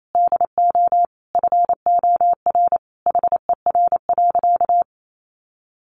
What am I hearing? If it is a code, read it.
Morse code, heard as a keyed tone, about 28 wpm, 700 Hz. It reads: DO FOR HER.
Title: Sentences from Top 600 Words 28wpm